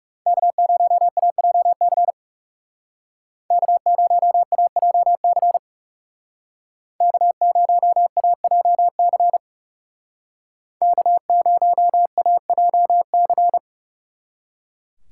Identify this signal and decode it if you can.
Morse code, heard as a keyed tone, about 36 wpm, 700 Hz. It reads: K0AJC K0AJC K0AJC K0AJC
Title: Speed Racing - USA 1x3 Call Signs 30wpm